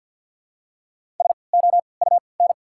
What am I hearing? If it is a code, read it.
Morse code, heard as a keyed tone, about 48 wpm, spaced as for 28 wpm, 700 Hz. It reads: SOUN
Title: 3x Character Spacing - Top 300 Words - 50wpm